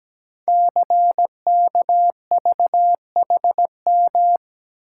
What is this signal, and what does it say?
Morse code, heard as a keyed tone, about 17 wpm, 700 Hz. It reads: CKVHM
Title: Code Groups - 5 Letters - 17wpm